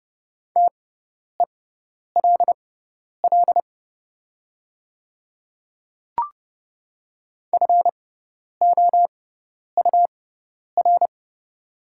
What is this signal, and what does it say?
Morse code, heard as a keyed tone, about 30 wpm, spaced as for 10 wpm, 700 Hz. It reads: TELL FOUR
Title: 6x Character Spacing - Top 300 Words - 30wpm